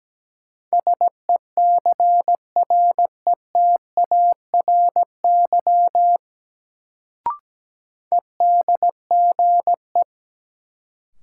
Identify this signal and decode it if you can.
Morse code, heard as a keyed tone, about 17 wpm, 700 Hz. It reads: SECRETARY EDGE
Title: Sentences from Top 1300 Words - New Word Drill Alt - 17wpm